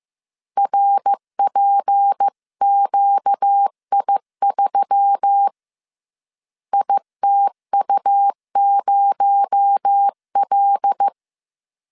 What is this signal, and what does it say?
Morse code, heard as a keyed tone, about 15 wpm, 800 Hz. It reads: RPQI3 ITU0L
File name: CW / DET / 1 6 1500C S C 14wpm 2951